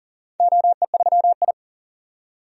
Morse code, heard as a keyed tone, about 42 wpm, 700 Hz. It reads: OE3I
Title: DXCC-100 Call Signs - 4 Characters - 40wpm